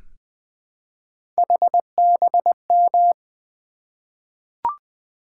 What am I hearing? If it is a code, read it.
Morse code, heard as a keyed tone, about 20 wpm, 700 Hz. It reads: HBM
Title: Three Letter-Number Combinations - 20wpm